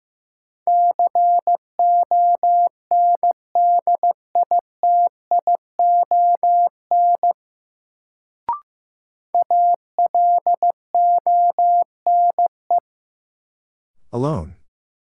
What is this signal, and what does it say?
Morse code, heard as a keyed tone, about 16 wpm, 700 Hz. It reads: CONDITION ALONE
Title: Sentences from Top 1000 Words - New Word Drill - 15wpm